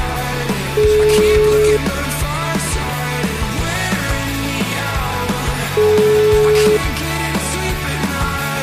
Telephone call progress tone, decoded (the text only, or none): ringback tone